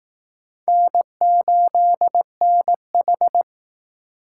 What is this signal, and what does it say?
Morse code, heard as a keyed tone, about 19 wpm, 700 Hz. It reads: N8NH